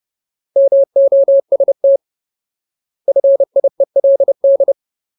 Morse code, heard as a keyed tone, about 30 wpm, 550 Hz. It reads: MOST FIELD
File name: Ultimate Word List / 30WPM Ultimate